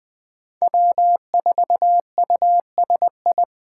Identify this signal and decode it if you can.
Morse code, heard as a keyed tone, about 20 wpm, 700 Hz. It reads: W4USI
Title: K1USN SST QSO Practice - 20wpm